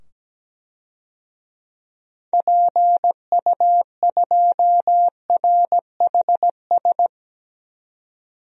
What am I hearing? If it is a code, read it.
Morse code, heard as a keyed tone, about 17 wpm, 700 Hz. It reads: PU2RHS
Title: DXCC-100 Call Signs - 6 Characters - 17wpm